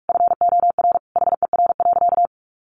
Morse code, heard as a keyed tone, about 44 wpm, 700 Hz. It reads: FOR HER.
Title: Sentences from Top 1200 Words 45wpm